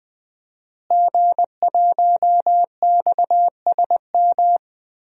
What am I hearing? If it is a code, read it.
Morse code, heard as a keyed tone, about 20 wpm, 700 Hz. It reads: G1XSM
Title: DXCC-100 Call Signs - 5 Characters - Repeated 2x - 20wpm